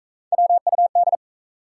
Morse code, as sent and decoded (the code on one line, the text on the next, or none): .-- ..- -..
WUD